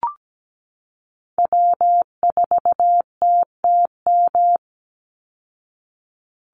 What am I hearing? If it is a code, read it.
Morse code, heard as a keyed tone, about 17 wpm, 700 Hz. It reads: W4TTM
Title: Call Signs Repeated 3x - 17wpm